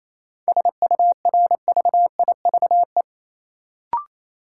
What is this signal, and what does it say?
Morse code, heard as a keyed tone, about 27 wpm, 700 Hz. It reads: SURVIVE
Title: Sentences from Top 1300 Words - New Word Drill - 28wpm